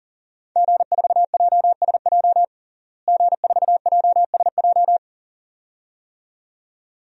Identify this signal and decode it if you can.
Morse code, heard as a keyed tone, about 37 wpm, 700 Hz. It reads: G4JSJ G4JSJ
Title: DXCC-100 Call Signs - 5 Characters - Repeated 2x - 40wpm